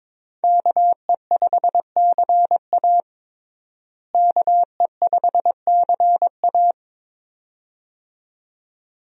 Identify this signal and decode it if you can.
Morse code, heard as a keyed tone, about 21 wpm, 700 Hz. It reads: KE5CA KE5CA
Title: USA Call Signs - 5 Characters - Repeated 2x - 22wpm